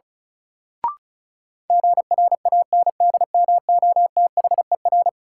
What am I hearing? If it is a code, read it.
Morse code, heard as a keyed tone, about 36 wpm, 700 Hz. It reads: GRANDMOTHER
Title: Sentences from Top 1200 Words - New Word Drill Alt - 35wpm